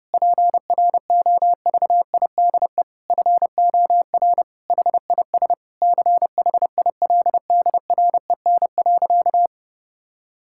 Morse code, heard as a keyed tone, about 30 wpm, 700 Hz. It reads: PROVIDE FOR HIS CHILDREN.
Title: Sentences from Top 1600 Words 30wpm